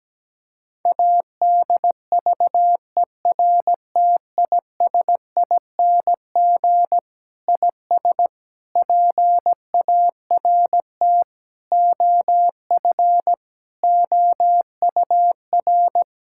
Morse code, heard as a keyed tone, about 17 wpm, 700 Hz. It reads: ADVERTISING IS PART OF OUR
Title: Sentences from Top 1000 Words - Review Part 6 - 17wpm